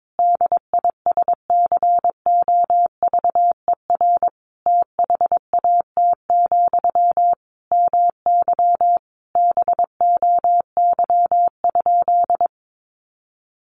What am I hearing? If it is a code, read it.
Morse code, heard as a keyed tone, about 22 wpm, 700 Hz. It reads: DISCOVER THAT, MY BOY?
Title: Sentences from Top 1100 Words 22wpm